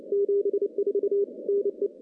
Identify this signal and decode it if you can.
Morse code, heard as a keyed tone, about 28 wpm, 400 Hz. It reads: 74 NE
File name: CW-SS-k5tr